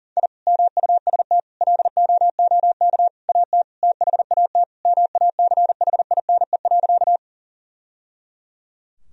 Morse code, heard as a keyed tone, about 40 wpm, 700 Hz. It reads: I MUST LOOK AT THAT MACHINE.
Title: Sentences from Top 1000 Words - Review Part 2 - 40wpm